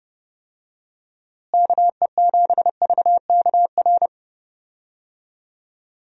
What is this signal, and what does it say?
Morse code, heard as a keyed tone, about 31 wpm, 700 Hz. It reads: KE7VKR
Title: USA Call Signs - 6 Characters - Repeated 3x - 30wpm